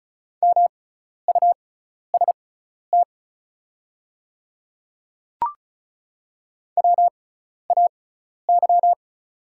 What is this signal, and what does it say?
Morse code, heard as a keyed tone, about 36 wpm, spaced as for 12 wpm, 690 Hz. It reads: MUST WAY
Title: 6x Character Spacing - Top 300 Words - 35wpm